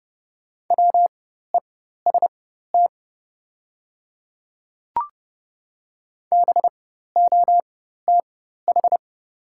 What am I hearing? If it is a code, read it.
Morse code, heard as a keyed tone, about 30 wpm, spaced as for 14 wpm, 690 Hz. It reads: WEST BOTH
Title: 4x Character Spacing - Top 500 Words - 30wpm